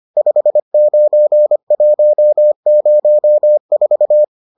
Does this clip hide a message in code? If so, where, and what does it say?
Yes: Morse code from 0.2 to 4.2 s, heard as a keyed tone, about 25 wpm, 600 Hz. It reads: 59104